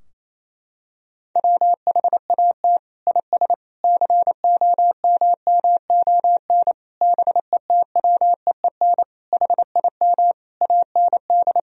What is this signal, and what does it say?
Morse code, heard as a keyed tone, about 28 wpm, 700 Hz. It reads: WHAT IS COMMON BETWEEN HIM AND
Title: Sentences from Top 1000 Words - Review Part 3 - 28wpm